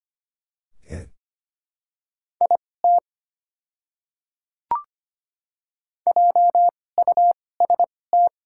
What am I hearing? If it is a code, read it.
Morse code, heard as a keyed tone, about 25 wpm, spaced as for 18 wpm, 700 Hz. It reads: IT JUST
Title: 2x Character Spacing - Top 500 Words - 25wpm